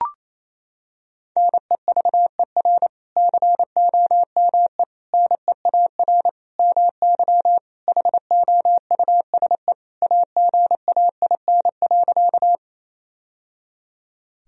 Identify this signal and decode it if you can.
Morse code, heard as a keyed tone, about 28 wpm, 700 Hz. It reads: NEVER COME NEAR MY HOUSE AGAIN.
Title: Sentences from Top 1000 Words - Review Part 1 - 28wpm